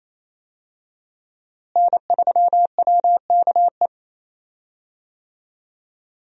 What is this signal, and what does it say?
Morse code, heard as a keyed tone, about 28 wpm, 700 Hz. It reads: N3WKE
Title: USA Call Signs - 5 Characters - Repeated 2x - 28wpm